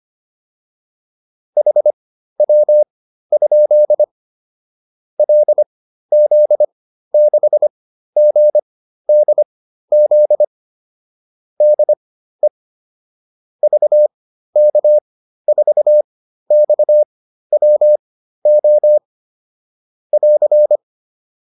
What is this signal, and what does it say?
Morse code, heard as a keyed tone, about 25 wpm, spaced as for 13 wpm, 600 Hz. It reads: HW? LZ6GDZ DE VK4XWO <AR>
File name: QSO 407 13